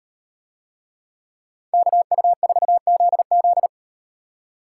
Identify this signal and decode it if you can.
Morse code, heard as a keyed tone, about 35 wpm, 700 Hz. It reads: KU4ZZ